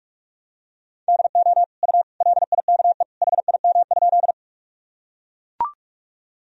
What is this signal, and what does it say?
Morse code, heard as a keyed tone, about 41 wpm, 700 Hz. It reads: DO U LIKE HIM?
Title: Sentences from Top 100 Words and Abbreviations 45wpm